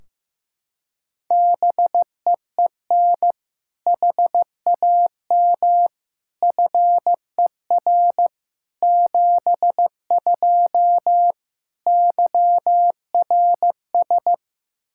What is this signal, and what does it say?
Morse code, heard as a keyed tone, about 15 wpm, 700 Hz. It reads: BEEN HAM FER 72 YRS
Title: Practice QSO - Exchange 3 - More Info - 15wpm